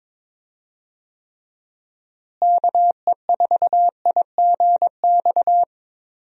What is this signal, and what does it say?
Morse code, heard as a keyed tone, about 22 wpm, 700 Hz. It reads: KE4IGX
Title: USA Call Signs - 6 Characters - Repeated 3x - 22wpm